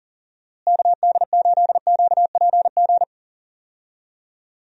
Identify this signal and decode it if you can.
Morse code, heard as a keyed tone, about 40 wpm, 700 Hz. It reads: KD8QPG
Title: USA Call Signs - 6 Characters - Repeated 3x - 40wpm